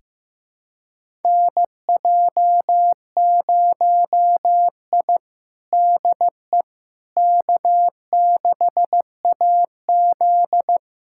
Morse code, heard as a keyed tone, about 16 wpm, 700 Hz. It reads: NJ0I DE K6AZ
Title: USA Callsign DE Callsign - 4 Characters - 15wpm